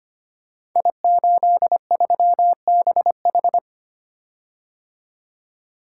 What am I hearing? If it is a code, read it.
Morse code, heard as a keyed tone, about 25 wpm, 700 Hz. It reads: I83BH
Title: Code Groups - 5 Letter:Numbers - 25wpm